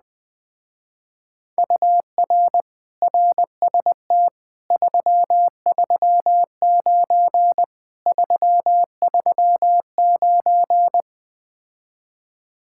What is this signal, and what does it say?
Morse code, heard as a keyed tone, about 20 wpm, 700 Hz. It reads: UR RST 339 339
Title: Speed-Racing Common QSO Phrases - 17wpm